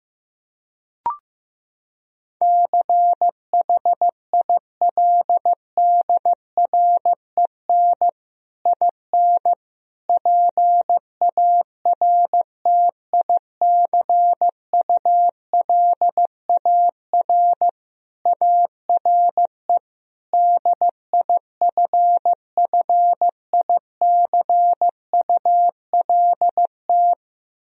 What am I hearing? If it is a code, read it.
Morse code, heard as a keyed tone, about 15 wpm, 700 Hz. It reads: CHILDREN IN PARTICULAR ARE DIFFICULT
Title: Sentences from Top 1000 Words - Review Part 4 - 15wpm